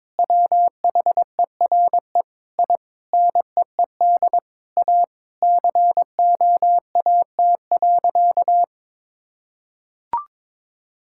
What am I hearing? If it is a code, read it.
Morse code, heard as a keyed tone, about 22 wpm, 700 Hz. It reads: WHERE I NEED A COAT.
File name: practice / 22wpm / Sentences with 4-Word Phrases - 22wpm